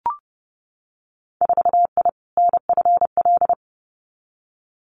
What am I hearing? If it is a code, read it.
Morse code, heard as a keyed tone, about 30 wpm, 700 Hz. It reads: 4I NFL
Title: Winter Field Day Exchanges 30wpm